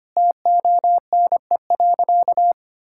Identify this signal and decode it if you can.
Morse code, heard as a keyed tone, about 25 wpm, 700 Hz. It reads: TONE.
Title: Sentences from Top 1600 Words 25wpm